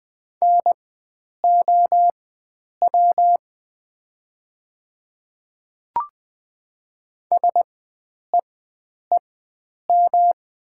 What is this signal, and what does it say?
Morse code, heard as a keyed tone, about 20 wpm, spaced as for 9 wpm, 700 Hz. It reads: NOW SEEM
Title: 4x Character Spacing - Top 500 Words - 20wpm